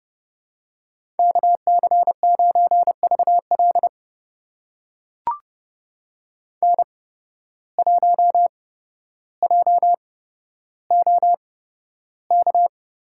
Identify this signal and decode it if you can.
Morse code, heard as a keyed tone, about 30 wpm, spaced as for 13 wpm, 700 Hz. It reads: KC9VL N1JOK
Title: USA Call Signs - 5 Characters - 8x Character Spacing - 30wpm